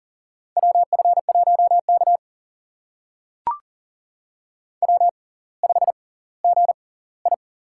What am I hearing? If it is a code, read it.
Morse code, heard as a keyed tone, about 39 wpm, spaced as for 20 wpm, 690 Hz. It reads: WF1K W5GI
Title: USA Call Signs - 4 Characters - 6x Character Spacing - 40wpm